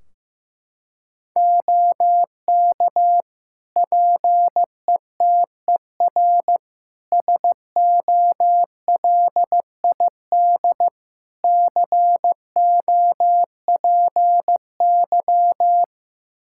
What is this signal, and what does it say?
Morse code, heard as a keyed tone, about 15 wpm, 700 Hz. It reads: OK PETER SOLID COPY
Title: K1USN SST QSO Practice 15wpm